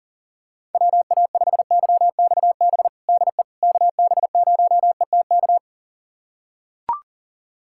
Morse code, heard as a keyed tone, about 37 wpm, 700 Hz. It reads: WA5YXB DE KB0ETK
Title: USA Callsign DE Callsign - 6 Characters - 40wpm